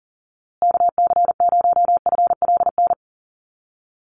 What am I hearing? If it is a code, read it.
Morse code, heard as a keyed tone, about 40 wpm, 700 Hz. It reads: KC0FLN
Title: USA Call Signs Repeated 3x 40wpm